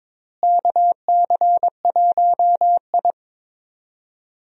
Morse code, heard as a keyed tone, about 21 wpm, 700 Hz. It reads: KC1I
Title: USA Call Signs - 4 Characters - Repeated 3x - 22wpm